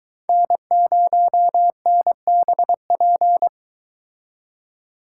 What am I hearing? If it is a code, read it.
Morse code, heard as a keyed tone, about 24 wpm, 700 Hz. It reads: N0NBP